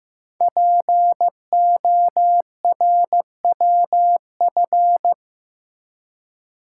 Morse code, heard as a keyed tone, about 15 wpm, 700 Hz. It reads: PORWF
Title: Code Groups - 5 Letters - 15wpm